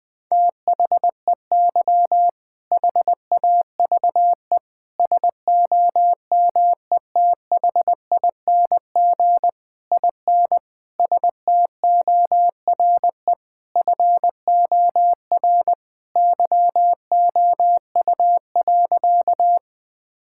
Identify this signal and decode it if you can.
Morse code, heard as a keyed tone, about 20 wpm, 700 Hz. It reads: THEY HAVE SOMETHING IN STORE FOR YOU.
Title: Sentences from Top 1000 Words - Review Part 5 - 20wpm